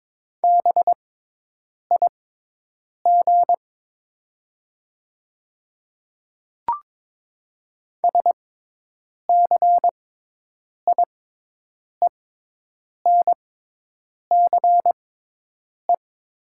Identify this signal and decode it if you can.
Morse code, heard as a keyed tone, about 22 wpm, spaced as for 8 wpm, 700 Hz. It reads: BIG SCIENCE
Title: 6x Character Spacing - Top 500 Words - 22wpm